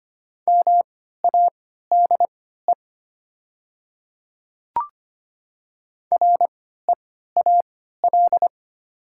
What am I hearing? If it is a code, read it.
Morse code, heard as a keyed tone, about 25 wpm, spaced as for 14 wpm, 700 Hz. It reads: MADE REAL
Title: 3x Character Spacing - Top 300 Words - 25wpm